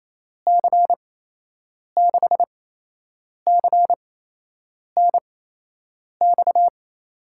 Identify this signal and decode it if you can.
Morse code, heard as a keyed tone, about 28 wpm, spaced as for 8 wpm, 700 Hz. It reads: C6CNX
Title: USA Call Signs - 6 Characters - 8x Character Spacing - 28wpm